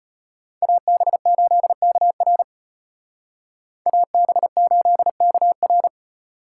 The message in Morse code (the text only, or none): AB8KR AB8KR